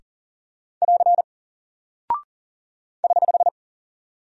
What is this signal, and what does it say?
Morse code, heard as a keyed tone, about 39 wpm, 700 Hz. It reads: <AR> <HH>
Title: Single Character (Letter:Number:Punctuation:Prosign) Alt 40wpm